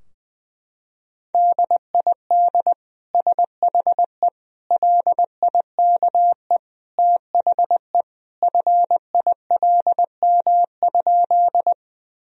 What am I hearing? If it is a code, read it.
Morse code, heard as a keyed tone, about 20 wpm, 700 Hz. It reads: DID SHE LIKE THE FILM?
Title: Sentences from Top 1000 Words - Review Part 3 - 20wpm